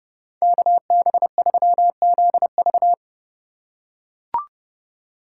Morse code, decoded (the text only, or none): KB3ZV